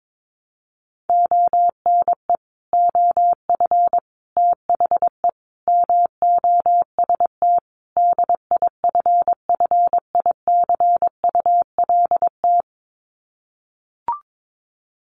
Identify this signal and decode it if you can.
Morse code, heard as a keyed tone, about 22 wpm, 700 Hz. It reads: ONE OF THE MOST DIFFICULT
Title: Sets of 5 Words Encore2 22wpm